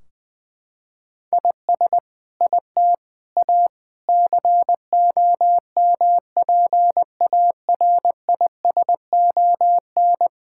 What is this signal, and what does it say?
Morse code, heard as a keyed tone, about 20 wpm, 700 Hz. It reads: IS IT A COMPARISON
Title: Sentences from Top 1600 Words 20wpm